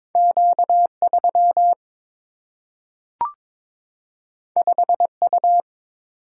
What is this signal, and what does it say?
Morse code, heard as a keyed tone, about 22 wpm, 700 Hz. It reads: Q3 5U
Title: Single-Letter Single-Number 22wpm